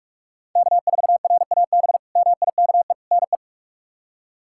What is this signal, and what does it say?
Morse code, heard as a keyed tone, about 46 wpm, 700 Hz. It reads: K4RAB MIKE NE
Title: K1USN SST QSO Practice - 45wpm